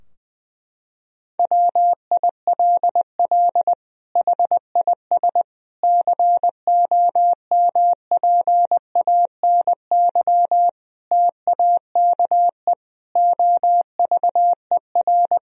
Morse code, heard as a keyed tone, about 20 wpm, 700 Hz. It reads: WILL HIS COMPANY TAKE OVER
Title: Sentences from Top 1000 Words - Review Part 4 - 20wpm